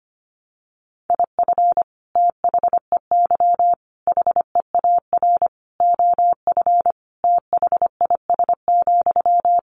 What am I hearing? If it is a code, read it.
Morse code, heard as a keyed tone, about 25 wpm, 700 Hz. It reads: IF THEY HEAR OF THIS,
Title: Sentences from Top 400 Words 25wpm